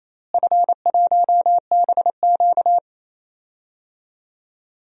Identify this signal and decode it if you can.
Morse code, heard as a keyed tone, about 27 wpm, 700 Hz. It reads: F1BQ
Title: DXCC-100 Call Signs - 4 Characters - Repeated 2x - 28wpm